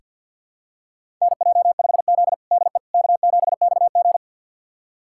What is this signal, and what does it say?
Morse code, heard as a keyed tone, about 48 wpm, 700 Hz. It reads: NJ5Z DE K7XG
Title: USA Callsign DE Callsign - 4 Characters - 50wpm